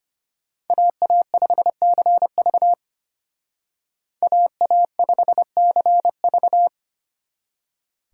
Morse code, heard as a keyed tone, about 33 wpm, 700 Hz. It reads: AA5CV AA5CV